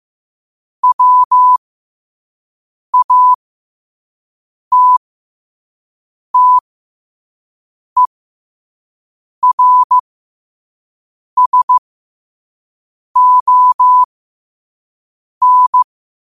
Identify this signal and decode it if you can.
Morse code, heard as a keyed tone, about 15 wpm, spaced as for 5 wpm, 1000 Hz. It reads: WATTERSON
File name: qotdfiles / 5wpm